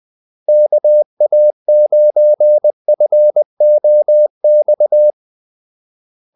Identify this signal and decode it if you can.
Morse code, heard as a keyed tone, about 20 wpm, 600 Hz. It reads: KA9FOX
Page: Kaldesignaler 20 wpm